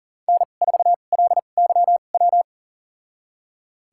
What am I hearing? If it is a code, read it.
Morse code, heard as a keyed tone, about 40 wpm, spaced as for 29 wpm, 700 Hz. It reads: N4LYW
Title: USA Call Signs - 5 Characters - 2x Character Spacing - 40wpm